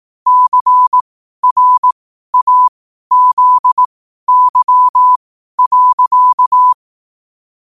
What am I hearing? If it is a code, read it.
Morse code, heard as a keyed tone, about 18 wpm, spaced as for 12 wpm, 1000 Hz. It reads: CRAZY.
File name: qotdfiles / 12wpm